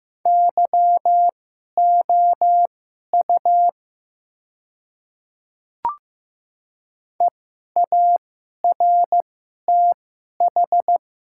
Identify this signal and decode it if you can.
Morse code, heard as a keyed tone, about 15 wpm, spaced as for 11 wpm, 700 Hz. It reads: YOU EARTH
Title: 2x Character Spacing - Top 200 Words - 15wpm